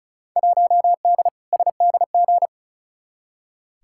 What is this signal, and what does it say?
Morse code, heard as a keyed tone, about 36 wpm, 700 Hz. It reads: 1D SDG
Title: Field Day Exchanges - 35wpm